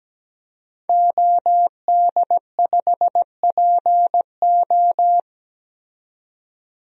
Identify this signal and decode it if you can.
Morse code, heard as a keyed tone, about 17 wpm, 700 Hz. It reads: OD5PO
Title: DXCC-100 Call Signs - 5 Characters - 17wpm